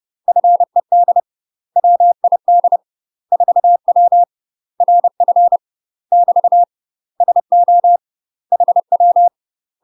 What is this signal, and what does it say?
Morse code, heard as a keyed tone, about 30 wpm, 700 Hz. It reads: FED WID 4W RF <BT> SO HW
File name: UK MASTER QSOs TRAINING / 30WPM / UK MASTER QSOs 0000